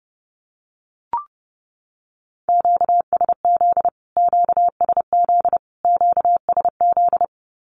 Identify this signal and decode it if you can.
Morse code, heard as a keyed tone, about 28 wpm, 700 Hz. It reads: QSZ QSZ QSZ
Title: Q-Codes Repeated 3x 30wpm